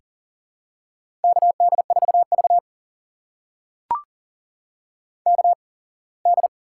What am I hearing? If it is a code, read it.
Morse code, heard as a keyed tone, about 44 wpm, spaced as for 25 wpm, 700 Hz. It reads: KD4V KD
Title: USA Call Signs - 8x Character Spacing - 40wpm